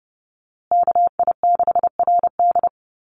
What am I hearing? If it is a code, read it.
Morse code, heard as a keyed tone, about 30 wpm, 700 Hz. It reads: KI6RD
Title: USA Call Signs Repeated 2x 30wpm